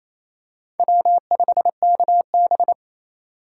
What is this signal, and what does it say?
Morse code, heard as a keyed tone, about 28 wpm, 700 Hz. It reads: W5KB
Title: USA Call Signs - 4 Characters - 2x Character Spacing - 28wpm